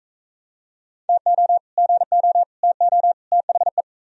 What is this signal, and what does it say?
Morse code, heard as a keyed tone, about 44 wpm, 700 Hz. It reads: TO GO TO THE